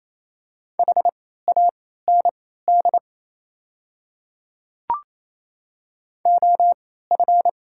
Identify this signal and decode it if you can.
Morse code, heard as a keyed tone, about 28 wpm, spaced as for 16 wpm, 700 Hz. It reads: HAND OF
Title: 3x Character Spacing - Top 200 Words - 28wpm